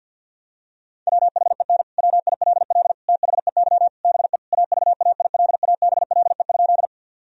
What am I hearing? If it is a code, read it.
Morse code, heard as a keyed tone, about 50 wpm, 700 Hz. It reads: WHEN WILL THEY BE AVAILABLE?
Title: Sentences from Top 1000 Words - Review Part 3 - 50wpm